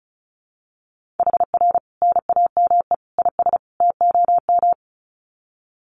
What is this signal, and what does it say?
Morse code, heard as a keyed tone, about 34 wpm, 700 Hz. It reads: HR NAME IS TOM
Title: Common QSO Phrases 35wpm